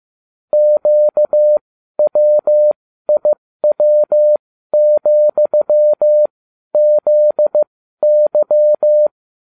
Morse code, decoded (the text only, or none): QWIW,ZY